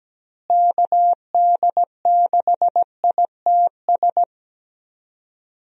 Morse code, heard as a keyed tone, about 17 wpm, 700 Hz. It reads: KD6ITS
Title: USA Call Signs - 6 Characters - Repeated 2x - 17wpm